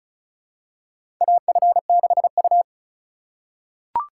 Morse code, heard as a keyed tone, about 36 wpm, 700 Hz. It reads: AF6U
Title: USA Call Signs - 4 Characters - 6x Character Spacing - 35wpm